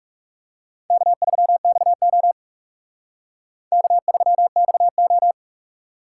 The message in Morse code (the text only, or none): K3XO K3XO